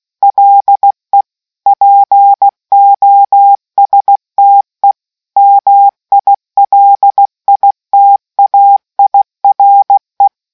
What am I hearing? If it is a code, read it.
Morse code, heard as a keyed tone, about 16 wpm, 800 Hz. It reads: LE POSTE MILITAIRE